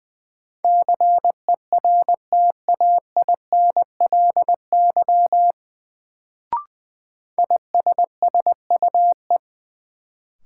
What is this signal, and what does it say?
Morse code, heard as a keyed tone, about 20 wpm, 700 Hz. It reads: CERTAINLY ISSUE
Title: Sentences from Top 900 Words - New Word Drill - 20wpm